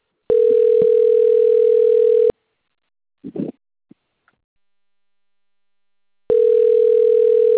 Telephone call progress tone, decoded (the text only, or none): ringback tone